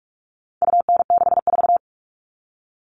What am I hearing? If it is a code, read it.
Morse code, heard as a keyed tone, about 44 wpm, 700 Hz. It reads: UN64
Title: All Cut Numbers - 4-Digit 45wpm